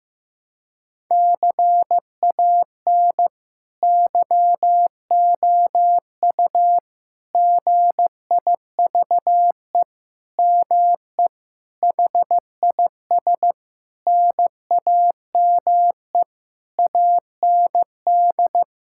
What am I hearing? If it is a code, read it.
Morse code, heard as a keyed tone, about 15 wpm, 700 Hz. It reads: CAN YOU GIVE ME HIS NAME AND
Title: Sentences from Top 1000 Words - Review Part 6 - 15wpm